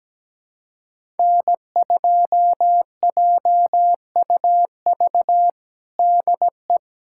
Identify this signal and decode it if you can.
Morse code, heard as a keyed tone, about 17 wpm, 700 Hz. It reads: N2JUV DE
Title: USA Callsign DE Callsign - 5 Characters - 17wpm